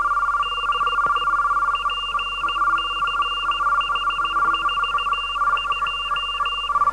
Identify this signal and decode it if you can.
Morse code, heard as a keyed tone, about 16 wpm, 2800 Hz, with noise. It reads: DE PCH42